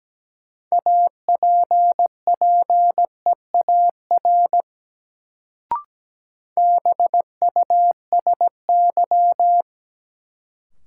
Morse code, heard as a keyed tone, about 17 wpm, 700 Hz. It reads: APPEAR BUSY
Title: Sentences from Top 1000 Words - Review Part 2 - New Word Drill - 17wpm